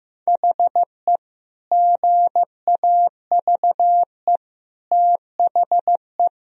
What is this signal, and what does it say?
Morse code, heard as a keyed tone, about 15 wpm, 700 Hz. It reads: HE GAVE THE
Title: Sentences from Top 1000 Words - Review Part 2 - 15wpm